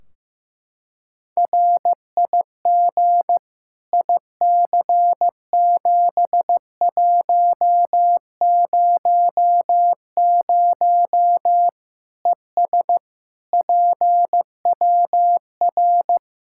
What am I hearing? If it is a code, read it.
Morse code, heard as a keyed tone, about 15 wpm, 700 Hz. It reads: RIG IC7100 ES PWR
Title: Practice QSO - Exchange 2 - Info - 15wpm